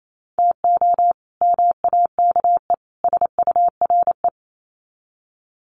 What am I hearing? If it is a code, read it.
Morse code, heard as a keyed tone, about 28 wpm, 700 Hz. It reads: TO MAKE SURE
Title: Sets of 3 Words 28wpm